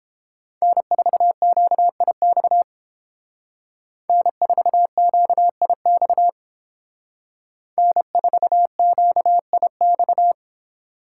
Morse code, heard as a keyed tone, about 31 wpm, 700 Hz. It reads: N4QIX N4QIX N4QIX